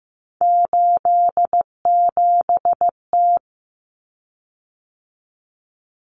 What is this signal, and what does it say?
Morse code, heard as a keyed tone, about 15 wpm, 700 Hz. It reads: 87T
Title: Common Cut Numbers - 3-Digit 15wpm